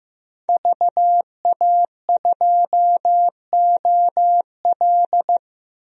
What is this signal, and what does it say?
Morse code, heard as a keyed tone, about 15 wpm, 700 Hz. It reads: VA2OL
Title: DXCC-100 Call Signs - 5 Characters - 15wpm